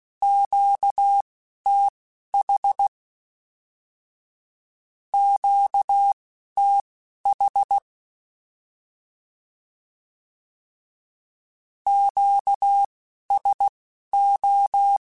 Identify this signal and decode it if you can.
Morse code, heard as a keyed tone, about 16 wpm, spaced as for 11 wpm, 800 Hz. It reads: QTH QTH QSO
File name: Q Calls / Q Calls-mp3s / Q Calls Twice @10